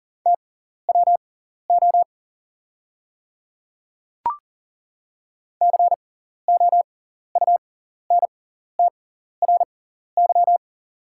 Morse code, heard as a keyed tone, about 40 wpm, spaced as for 14 wpm, 690 Hz. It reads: TWO COUNTRY
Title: 6x Character Spacing - Top 300 Words - 40wpm